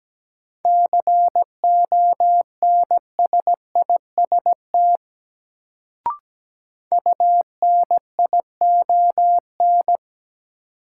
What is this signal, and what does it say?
Morse code, heard as a keyed tone, about 17 wpm, 700 Hz. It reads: CONSIST UNION
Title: Sentences from Top 1400 Words - New Word Drill - 17wpm